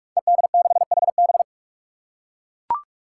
Morse code, heard as a keyed tone, about 44 wpm, 700 Hz. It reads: ED6HB
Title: Five Challenging Letter-Number Combinations 45wpm